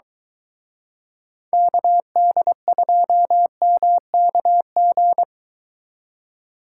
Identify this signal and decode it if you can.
Morse code, heard as a keyed tone, about 23 wpm, 700 Hz. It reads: KD2MKG